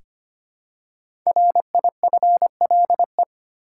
Morse code, heard as a keyed tone, about 25 wpm, 700 Hz. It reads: RIFLE
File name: 5 Letter Words - First 3 Letters Given - 25wpm